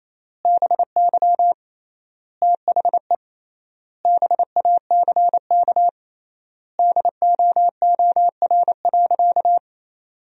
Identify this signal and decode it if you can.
Morse code, heard as a keyed tone, about 28 wpm, 700 Hz. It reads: BY THE BACK DOOR.
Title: 3x Word-Spacing - Sentences from Top 300 Words - 28wpm